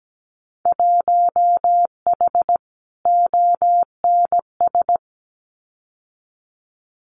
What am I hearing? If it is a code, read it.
Morse code, heard as a keyed tone, about 17 wpm, 700 Hz. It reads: 1H ONS
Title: Winter Field Day Exchanges 17wpm